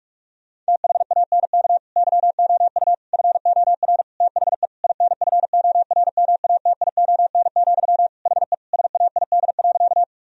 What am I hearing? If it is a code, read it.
Morse code, heard as a keyed tone, about 45 wpm, 700 Hz. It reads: THANK YOU FOR THE INFORMATION, HE SAID.
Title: Sentences from Top 1000 Words - Review Part 3 - 45wpm